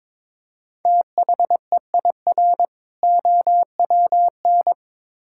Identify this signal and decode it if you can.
Morse code, heard as a keyed tone, about 22 wpm, 700 Hz. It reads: THEIR OWN
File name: Build A Phrase - 2-Word Phrases - 22wpm